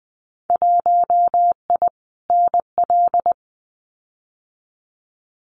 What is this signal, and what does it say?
Morse code, heard as a keyed tone, about 20 wpm, 700 Hz. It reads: 1I NL
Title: Winter Field Day Exchanges 20wpm